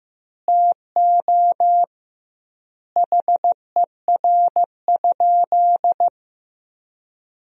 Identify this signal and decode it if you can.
Morse code, heard as a keyed tone, about 15 wpm, 700 Hz. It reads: TO HER?
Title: 2x Word-Spacing - Sentences from Top 200 Words - 15wpm